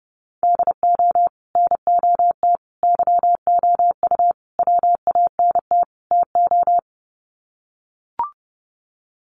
Morse code, heard as a keyed tone, about 30 wpm, 700 Hz. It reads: DO NOT YOU WANT TO
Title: Sets of 5 Words Encore2 30wpm